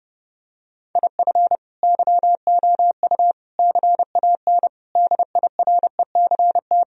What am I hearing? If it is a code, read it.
Morse code, heard as a keyed tone, about 32 wpm, 700 Hz. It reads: IF YOU CAN DIRECT
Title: Sentences from Top 1000 Words - Review Part 2 - 30wpm